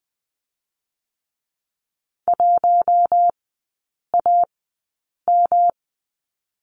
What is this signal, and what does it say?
Morse code, heard as a keyed tone, about 20 wpm, spaced as for 8 wpm, 700 Hz. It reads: 1AM
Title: 107 Lesson 18e - Letter m - Three Characters 20wpm